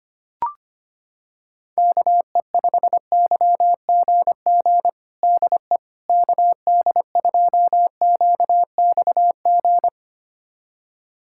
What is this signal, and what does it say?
Morse code, heard as a keyed tone, about 25 wpm, 700 Hz. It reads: KE5YGG DE KD2QXG
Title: USA Callsign DE Callsign - 6 Characters - 25wpm